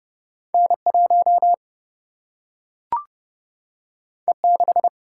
Morse code, heard as a keyed tone, about 29 wpm, 700 Hz. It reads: N 1 E 6
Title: Single-Letter Single-Number 30wpm